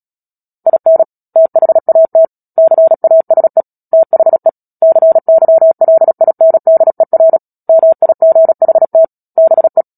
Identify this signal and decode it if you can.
Morse code, heard as a keyed tone, about 37 wpm, 650 Hz. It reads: IN THAT CASE THE CYLINDER MIGHT BE
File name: Book Two / 10-Chapter 10 - Epilogue - 36 WPM